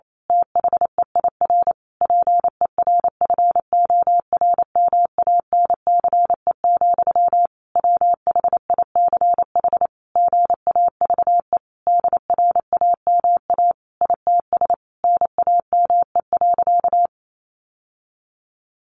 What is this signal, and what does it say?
Morse code, heard as a keyed tone, about 28 wpm, 700 Hz. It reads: THEIR PERFORMANCE, WHICH GAVE DRAMA ITS NAME.
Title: Sentences from Top 1500 Words 28wpm